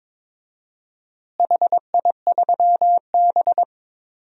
Morse code, heard as a keyed tone, about 21 wpm, 700 Hz. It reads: HI3B
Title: DXCC-100 Call Signs - 4 Characters - Repeated 2x - 22wpm